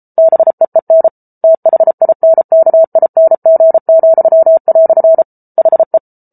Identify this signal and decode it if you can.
Morse code, heard as a keyed tone, about 34 wpm, 650 Hz. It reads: BEEN THINKING," HE